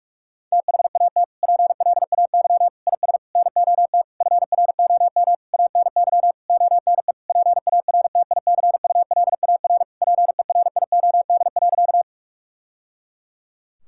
Morse code, heard as a keyed tone, about 47 wpm, 700 Hz. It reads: THAT PLAY IS NOT FROM ANY ONE PARTICULAR PERIOD.
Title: Sentences from Top 1000 Words - Review Part 5 - 45wpm